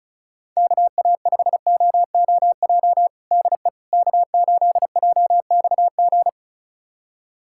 Morse code, heard as a keyed tone, about 36 wpm, 700 Hz. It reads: KA5OOJ DE K8JXG
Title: USA Callsign DE Callsign 35wpm